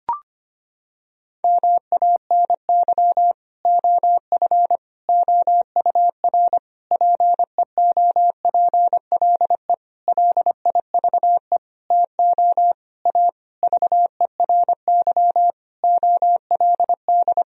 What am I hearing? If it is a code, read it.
Morse code, heard as a keyed tone, about 25 wpm, 700 Hz. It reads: MANY OF OUR PEOPLE LIVE TO A VERY OLD
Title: Sentences from Top 1000 Words - Review Part 3 - 25wpm